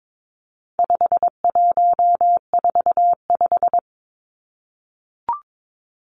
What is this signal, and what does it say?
Morse code, heard as a keyed tone, about 22 wpm, 700 Hz. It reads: 5145
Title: Numbers - 4-Digit 22wpm